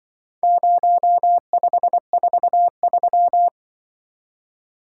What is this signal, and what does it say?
Morse code, heard as a keyed tone, about 23 wpm, 700 Hz. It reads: 0543
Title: Speed-Racing - Four Letter-Number Combinations - 20wpm